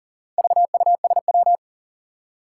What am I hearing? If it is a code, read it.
Morse code, heard as a keyed tone, about 40 wpm, 700 Hz. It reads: VUSW
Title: Four Challenging Letter-Number Combinations 40wpm